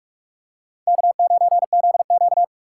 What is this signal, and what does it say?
Morse code, heard as a keyed tone, about 46 wpm, 700 Hz. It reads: K9ZQ
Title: USA Call Signs - 3x Character Spacing - 45wpm